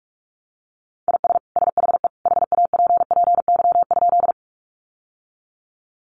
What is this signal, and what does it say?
Morse code, heard as a keyed tone, about 44 wpm, 700 Hz. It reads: IS SHE HAPPY?
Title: Sentences from Top 1000 Words 45wpm